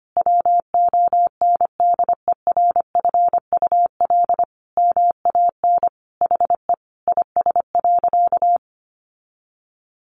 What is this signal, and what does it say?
Morse code, heard as a keyed tone, about 25 wpm, 700 Hz. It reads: WONDERFUL MAN HE IS.
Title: Sentences from Top 1100 Words 25wpm